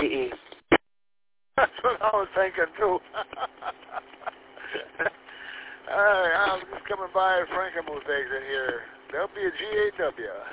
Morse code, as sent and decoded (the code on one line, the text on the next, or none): none